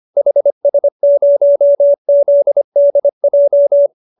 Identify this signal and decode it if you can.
Morse code, heard as a keyed tone, about 25 wpm, 590 Hz. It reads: HS0ZDJ